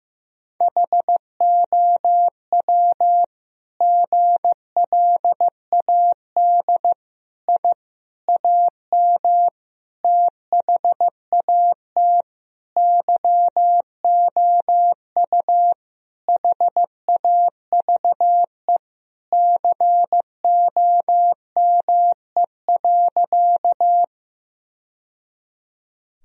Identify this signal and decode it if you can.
Morse code, heard as a keyed tone, about 15 wpm, 700 Hz. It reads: HOW GLAD I AM THAT YOU HAVE COME.
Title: Sentences from Top 1000 Words - Review Part 2 - 15wpm